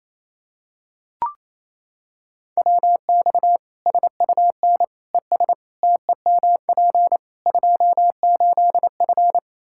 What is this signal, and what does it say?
Morse code, heard as a keyed tone, about 28 wpm, 700 Hz. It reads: WX SUN ES TEMP 28F
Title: Practice QSO - Exchange 2 - Info - 28wpm